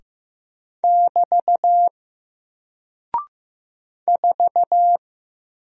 Morse code, heard as a keyed tone, about 15 wpm, 700 Hz. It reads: <BT> 4
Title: Single Character (Letter:Number:Punctuation:Prosign) 15wpm